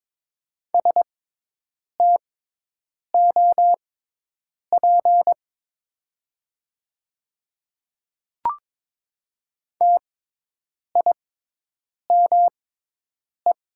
Morse code, heard as a keyed tone, about 22 wpm, spaced as for 8 wpm, 700 Hz. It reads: STOP TIME
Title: 6x Character Spacing - Top 300 Words - 22wpm